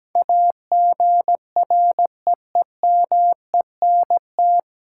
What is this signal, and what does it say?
Morse code, heard as a keyed tone, about 17 wpm, 700 Hz. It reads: AGREEMENT
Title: Sentences from Top 1500 Words - New Word Drill Alt - 17wpm